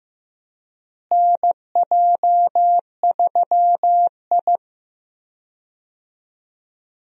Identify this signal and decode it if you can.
Morse code, heard as a keyed tone, about 15 wpm, 700 Hz. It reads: NJ3I
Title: USA Call Signs - 4 Characters - Repeated 2x - 15wpm